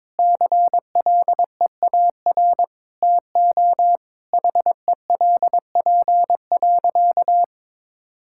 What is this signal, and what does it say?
Morse code, heard as a keyed tone, about 22 wpm, 700 Hz. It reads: CLEAR TO HELP.
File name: practice / 22wpm / Sentences with 8-Word Phrases - 22wpm